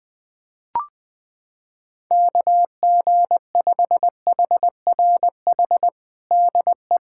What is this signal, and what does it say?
Morse code, heard as a keyed tone, about 20 wpm, 700 Hz. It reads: KG5HRH DE
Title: USA Callsign DE Callsign - 6 Characters - 20wpm